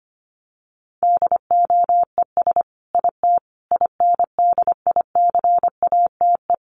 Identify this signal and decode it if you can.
Morse code, heard as a keyed tone, about 25 wpm, 700 Hz. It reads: DOES IT INDICATE
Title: Sentences from Top 1300 Words 25wpm